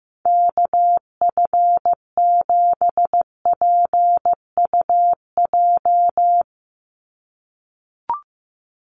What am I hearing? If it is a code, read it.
Morse code, heard as a keyed tone, about 15 wpm, 700 Hz. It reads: KF7PUJ
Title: USA Call Signs 15wpm